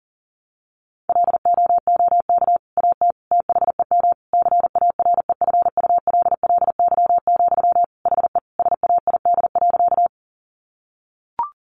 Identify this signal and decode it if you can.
Morse code, heard as a keyed tone, about 40 wpm, 700 Hz. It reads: LOOK AT THEM CAREFULLY, HE SAID.
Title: Sentences from Top 1300 Words 40wpm